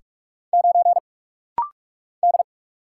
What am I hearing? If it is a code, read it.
Morse code, heard as a keyed tone, about 44 wpm, 700 Hz. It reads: 9 D
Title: Single Letter-Number - Rapid-Fire 45wpm